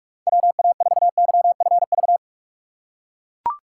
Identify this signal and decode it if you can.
Morse code, heard as a keyed tone, about 41 wpm, 700 Hz. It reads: WA4YFV
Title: USA Call Signs - 6 Characters - 3x Character Spacing - 45wpm